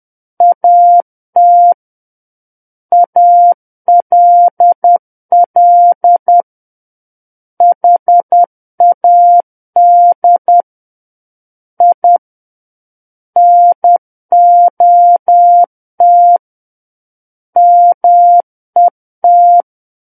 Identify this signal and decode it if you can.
Morse code, heard as a keyed tone, about 10 wpm, 700 Hz. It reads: AT ALL HAD I NOT MET
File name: Book One / 01-Chapter 1 - 10 WPM